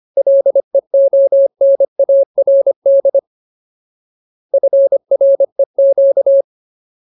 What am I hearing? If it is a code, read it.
Morse code, heard as a keyed tone, about 25 wpm, 550 Hz. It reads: LEONARD FREQ
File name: Ultimate Word List / 25WPM Ultimate